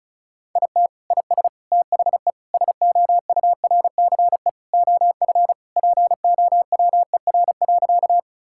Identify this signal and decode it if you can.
Morse code, heard as a keyed tone, about 34 wpm, 700 Hz. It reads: IT IS THE SOURCE OF POWER.
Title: Sentences from Top 1000 Words - Review Part 5 - 35wpm